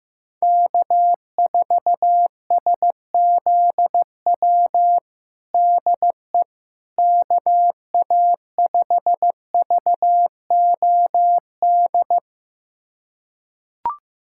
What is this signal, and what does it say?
Morse code, heard as a keyed tone, about 15 wpm, 700 Hz. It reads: K4SZW DE KA5VOD
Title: USA Callsign DE Callsign 15wpm